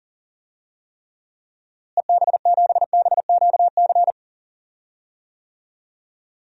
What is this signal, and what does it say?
Morse code, heard as a keyed tone, about 40 wpm, 700 Hz. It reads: EB7BQC
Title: DXCC-100 Call Signs - 6 Characters - Repeated 2x - 40wpm